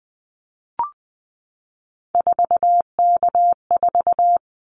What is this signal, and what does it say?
Morse code, heard as a keyed tone, about 20 wpm, 700 Hz. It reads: 4K4
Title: Callsign Prefixes for Top 100 DXCC - 20wpm